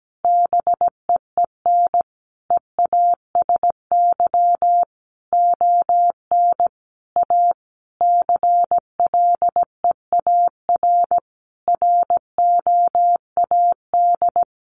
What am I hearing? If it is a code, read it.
Morse code, heard as a keyed tone, about 17 wpm, 700 Hz. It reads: BEEN EASY ON A CLEAR ROAD
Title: Sentences from Top 800 Words 17wpm